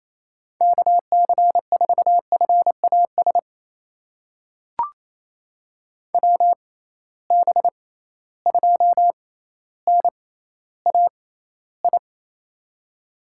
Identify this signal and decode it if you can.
Morse code, heard as a keyed tone, about 28 wpm, spaced as for 14 wpm, 700 Hz. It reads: KC4FAS WB2NAI
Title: USA Call Signs - 6 Characters - 6x Character Spacing - 28wpm